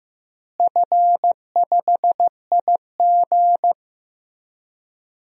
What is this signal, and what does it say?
Morse code, heard as a keyed tone, about 15 wpm, 700 Hz. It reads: F5IG
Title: DXCC-100 Call Signs - 4 Characters - Repeated 3x - 15wpm